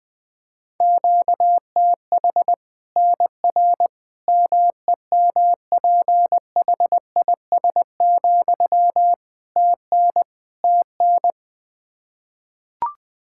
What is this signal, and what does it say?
Morse code, heard as a keyed tone, about 20 wpm, 700 Hz. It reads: QTH NR MEMPHIS, TN TN
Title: Practice QSO - Exchange 1 - Beginning - 20wpm